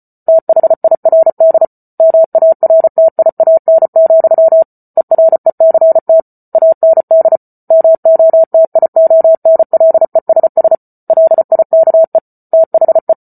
Morse code, heard as a keyed tone, about 34 wpm, 650 Hz. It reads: THIRD MARTIAN, ERECT AND MOTIONLESS LIKE THE
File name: Book Two / 08-Chapter 8 - 34 WPM